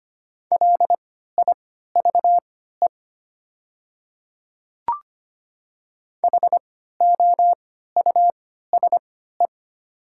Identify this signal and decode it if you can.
Morse code, heard as a keyed tone, about 25 wpm, spaced as for 14 wpm, 700 Hz. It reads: LIVE HOUSE
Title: 3x Character Spacing - Top 300 Words - 25wpm